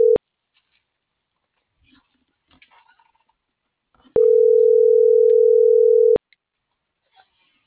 Ringback tone (North America), 440 + 480 Hz.